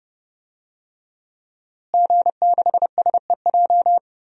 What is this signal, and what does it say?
Morse code, heard as a keyed tone, about 30 wpm, 700 Hz. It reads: G6SEJ
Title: DXCC-100 Call Signs - 5 Characters - Repeated 2x - 30wpm